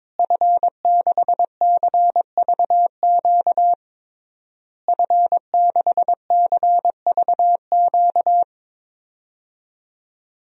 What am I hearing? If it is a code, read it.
Morse code, heard as a keyed tone, about 22 wpm, 700 Hz. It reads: F6CVQ F6CVQ
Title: DXCC-100 Call Signs - 5 Characters - Repeated 3x - 22wpm